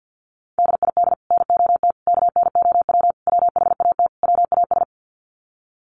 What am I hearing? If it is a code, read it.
Morse code, heard as a keyed tone, about 50 wpm, 700 Hz. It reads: DID NOT KNOW WHAT WAS
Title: Sets of 5 Words Encore2 50wpm